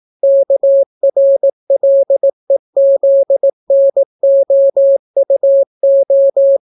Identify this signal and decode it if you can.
Morse code, heard as a keyed tone, about 18 wpm, 550 Hz. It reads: KRLEZNOUO